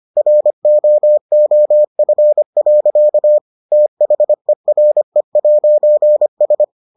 Morse code, heard as a keyed tone, about 25 wpm, 600 Hz. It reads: ROOF. THERE'S